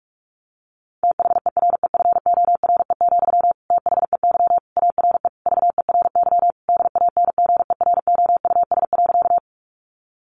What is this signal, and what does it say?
Morse code, heard as a keyed tone, about 44 wpm, 700 Hz. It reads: THEREFORE, THEY ARE VERY DANGEROUS.
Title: Sentences from Top 1200 Words 45wpm